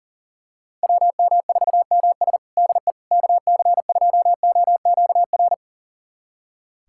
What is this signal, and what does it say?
Morse code, heard as a keyed tone, about 40 wpm, 700 Hz. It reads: WM4MS DE KC2OQR
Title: USA Callsign DE Callsign 40wpm